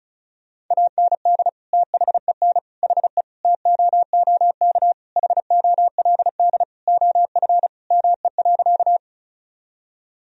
Morse code, heard as a keyed tone, about 34 wpm, 700 Hz. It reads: AND THEN HE TOOK HOLD OF ME.
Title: Sentences from Top 1000 Words - Review Part 2 - 35wpm